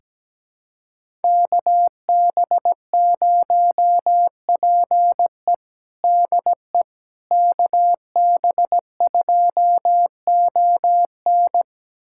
Morse code, heard as a keyed tone, about 17 wpm, 700 Hz. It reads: KB0PE DE KB2ON
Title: USA Callsign DE Callsign - 5 Characters - 17wpm